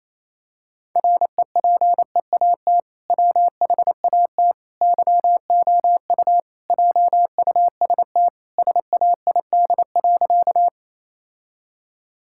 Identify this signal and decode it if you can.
Morse code, heard as a keyed tone, about 28 wpm, 700 Hz. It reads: REPEAT WHAT YOU JUST SAID.
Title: Sentences from Top 1600 Words 28wpm